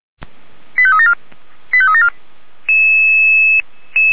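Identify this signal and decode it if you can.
Five-tone selective call (ZVEI), heard as quick runs of five tones, tones of about 70 ms. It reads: 85374 85374
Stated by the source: ZVEI five-tone selective call received by SDR, archived by the Orlamünde fire brigade (Feuerwehr Orlamünde).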